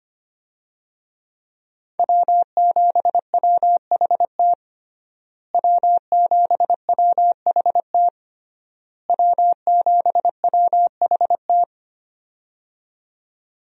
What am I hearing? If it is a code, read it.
Morse code, heard as a keyed tone, about 25 wpm, 700 Hz. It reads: W7WHT W7WHT W7WHT
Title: USA Call Signs - 5 Characters - Repeated 3x - 25wpm